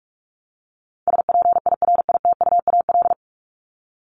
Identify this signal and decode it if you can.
Morse code, heard as a keyed tone, about 44 wpm, 700 Hz. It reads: SPIRITUAL
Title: Top 1501-1750 Words 45wpm